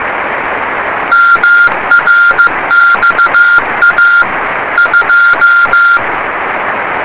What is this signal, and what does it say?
Morse code, heard as a keyed tone, about 15 wpm, 1400 Hz. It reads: MRXA 2
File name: rs30 cw